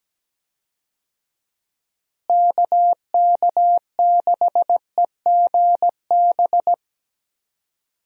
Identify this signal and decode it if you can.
Morse code, heard as a keyed tone, about 17 wpm, 700 Hz. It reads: KK6EGB
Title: USA Call Signs - 6 Characters - Repeated 3x - 17wpm